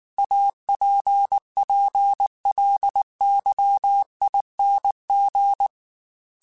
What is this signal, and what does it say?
Morse code, heard as a keyed tone, about 19 wpm, 800 Hz. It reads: APPLYING
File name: Technical Subjects / Technical Subjects-mp3s / Sw Pwr Sup @18